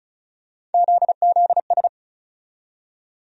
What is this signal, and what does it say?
Morse code, heard as a keyed tone, about 36 wpm, 700 Hz. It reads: ZZS
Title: USA Call Sign Postfixes - 3 Letters 35wpm